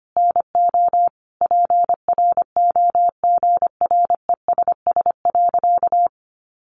Morse code, heard as a keyed tone, about 25 wpm, 700 Hz. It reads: NO PROGRESS.
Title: Sentences from Top 1400 Words 25wpm